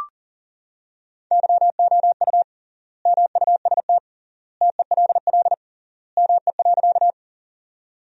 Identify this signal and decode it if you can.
Morse code, heard as a keyed tone, about 40 wpm, 700 Hz. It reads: YOU MUST TELL ME.
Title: 3x Word-Spacing - Sentences from Top 200 Words - 40wpm